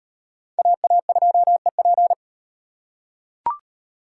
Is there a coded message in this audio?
Yes: Morse code from 0.6 to 2.1 s, heard as a keyed tone, about 38 wpm, 700 Hz. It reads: AA2EP